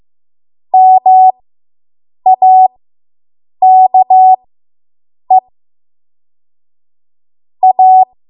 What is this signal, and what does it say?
Morse code, heard as a keyed tone, about 15 wpm, spaced as for 7 wpm, 750 Hz. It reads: MAKE A